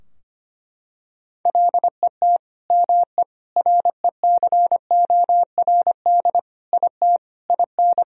Morse code, heard as a keyed tone, about 25 wpm, 700 Hz. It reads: LET ME RECORD IT IN
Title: Sentences from Top 1000 Words - Review Part 3 - 25wpm